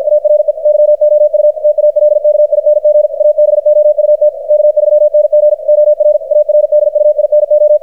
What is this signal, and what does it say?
Morse code, heard as a keyed tone, about 52 wpm, 600 Hz, with radio noise. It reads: ARE YOU TALKING ABOUT K3NG OR ARDUINO
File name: archived / CRSnet